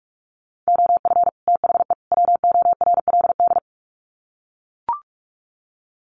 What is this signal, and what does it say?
Morse code, heard as a keyed tone, about 44 wpm, 700 Hz. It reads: OF THE WORLD
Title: Sets of 3 Words 45wpm